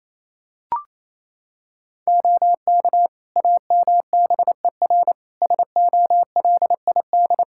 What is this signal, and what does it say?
Morse code, heard as a keyed tone, about 28 wpm, 700 Hz. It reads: OK AMBER SOLID
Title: Practice QSO - Exchange 3 - More Info - 28wpm